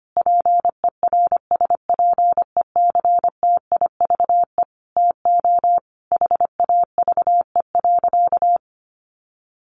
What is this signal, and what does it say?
Morse code, heard as a keyed tone, about 25 wpm, 700 Hz. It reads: PERSPECTIVE TO HAVE.
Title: Sentences from Top 1200 Words 25wpm